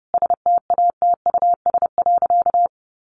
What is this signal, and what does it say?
Morse code, heard as a keyed tone, about 30 wpm, 700 Hz. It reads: STATUS.
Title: Sentences from Top 1400 Words 30wpm